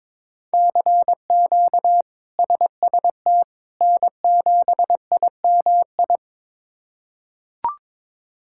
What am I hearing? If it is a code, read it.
Morse code, heard as a keyed tone, about 22 wpm, 700 Hz. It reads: CQ SST N7IMI
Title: K1USN SST QSO Practice - 22wpm